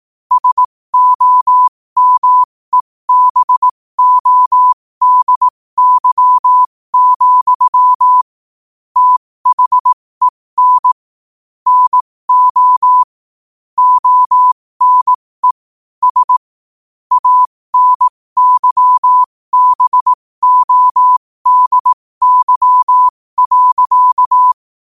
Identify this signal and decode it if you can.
Morse code, heard as a keyed tone, about 18 wpm, spaced as for 15 wpm, 1000 Hz. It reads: SOMEBODY, THEN NO ONE S ANYBODY.
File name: qotdfiles / 15wpm